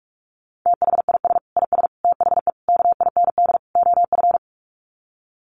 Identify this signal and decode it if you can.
Morse code, heard as a keyed tone, about 44 wpm, 700 Hz. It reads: THIS IS THE KIND OF
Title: Sets of 5 Words Encore 45wpm